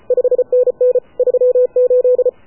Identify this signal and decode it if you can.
Morse code, heard as a keyed tone, about 33 wpm, 500 Hz. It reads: 5NN 38
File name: AUDIO2019 / HA7UI CW